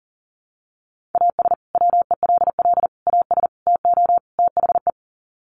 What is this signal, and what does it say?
Morse code, heard as a keyed tone, about 37 wpm, 700 Hz. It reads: AS WELL AS TO THE
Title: Sets of 5 Words Encore2 40wpm